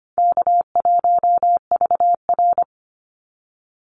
Morse code, heard as a keyed tone, about 25 wpm, 700 Hz. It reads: K1VR
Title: Call Signs Repeated 2x - 25wpm